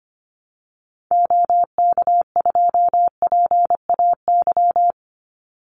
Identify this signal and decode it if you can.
Morse code, heard as a keyed tone, about 25 wpm, 700 Hz. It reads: OK2PAY
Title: Call Signs - 25wpm